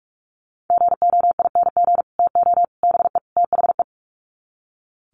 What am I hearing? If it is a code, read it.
Morse code, heard as a keyed tone, about 44 wpm, 700 Hz. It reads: GOING TO BE THE
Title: Sets of 4 Words 45wpm